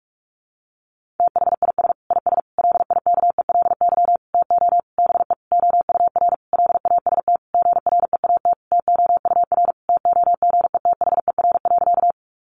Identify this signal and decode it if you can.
Morse code, heard as a keyed tone, about 44 wpm, 700 Hz. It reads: THIS IS LIKELY TO BE OUR LAST GREAT TOUR TOGETHER.
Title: Sentences from Top 1400 Words 45wpm